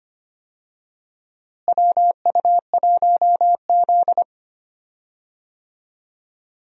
Morse code, heard as a keyed tone, about 25 wpm, 700 Hz. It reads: WU1Z
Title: USA Call Signs - 4 Characters - Repeated 3x - 25wpm